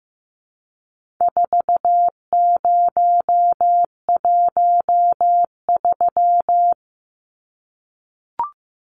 Morse code, heard as a keyed tone, about 15 wpm, 700 Hz. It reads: 4013
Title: Numbers - 4-Digit 15wpm